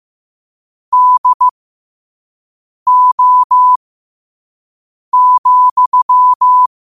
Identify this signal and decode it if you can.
Morse code, heard as a keyed tone, about 15 wpm, spaced as for 5 wpm, 1000 Hz. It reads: DO,
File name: qotdfiles / 5wpm